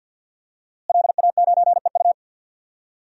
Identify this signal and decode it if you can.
Morse code, heard as a keyed tone, about 51 wpm, 700 Hz. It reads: LA9EU
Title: DXCC-100 Call Signs - 5 Characters - Repeated 2x - 50wpm